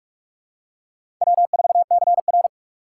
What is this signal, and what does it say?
Morse code, heard as a keyed tone, about 41 wpm, 700 Hz. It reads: W4CR
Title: USA Call Signs - 4 Characters - 3x Character Spacing - 45wpm